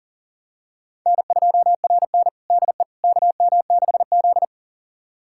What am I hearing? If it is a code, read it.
Morse code, heard as a keyed tone, about 37 wpm, 700 Hz. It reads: N2RN DE KM6Z
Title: USA Callsign DE Callsign - 4 Characters - 40wpm